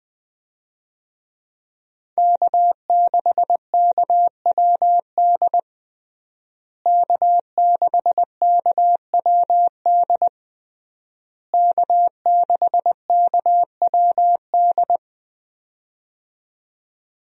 Morse code, heard as a keyed tone, about 20 wpm, 700 Hz. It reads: K6KWD K6KWD K6KWD
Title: USA Call Signs - 5 Characters - Repeated 3x - 20wpm